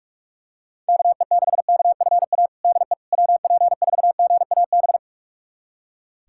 Morse code, heard as a keyed tone, about 41 wpm, 700 Hz. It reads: KE6KFA DE WP4GAB
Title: USA Callsign DE Callsign - 6 Characters - 45wpm